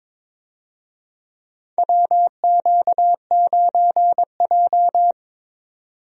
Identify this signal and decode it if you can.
Morse code, heard as a keyed tone, about 22 wpm, 700 Hz. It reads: WQ9J
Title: USA Call Signs - 4 Characters - 22wpm